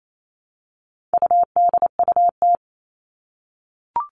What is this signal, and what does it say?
Morse code, heard as a keyed tone, about 28 wpm, 700 Hz. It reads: UDUT
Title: All Cut Numbers - 4-Digit 28wpm